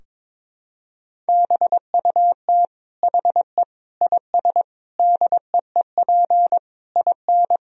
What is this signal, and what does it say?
Morse code, heard as a keyed tone, about 22 wpm, 700 Hz. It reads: BUT HE IS DEEP IN
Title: Sentences from Top 1000 Words - Review Part 3 - 22wpm